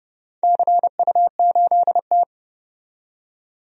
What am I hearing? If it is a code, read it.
Morse code, heard as a keyed tone, about 30 wpm, 700 Hz. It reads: CU8T
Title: DXCC-100 Call Signs - 4 Characters - 30wpm